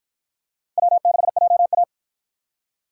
Morse code, heard as a keyed tone, about 51 wpm, 700 Hz. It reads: W6JA